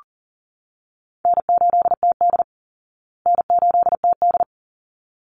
Morse code, heard as a keyed tone, about 40 wpm, 700 Hz. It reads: N8TD N8TD
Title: Easy Call Signs Repeated 2x - 40wpm